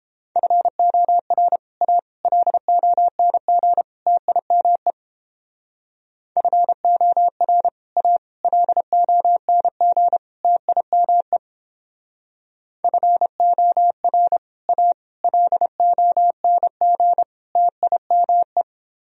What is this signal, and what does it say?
Morse code, heard as a keyed tone, about 29 wpm, 700 Hz. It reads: FOR A LONG TIME FOR A LONG TIME FOR A LONG TIME